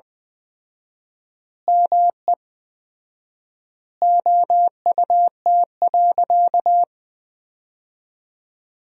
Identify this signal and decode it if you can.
Morse code, heard as a keyed tone, about 20 wpm, 700 Hz. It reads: ME OUT.
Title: 4x Word-Spacing - Sentences from Top 500 Words - 20wpm